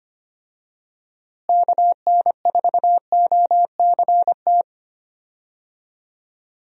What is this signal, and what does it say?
Morse code, heard as a keyed tone, about 25 wpm, 700 Hz. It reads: KN4OCT
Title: USA Call Signs - 6 Characters - Repeated 3x - 25wpm